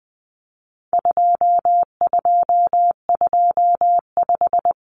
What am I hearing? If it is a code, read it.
Morse code, heard as a keyed tone, about 20 wpm, 700 Hz. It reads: 2225
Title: Common Cut Numbers - 4-Digit 20wpm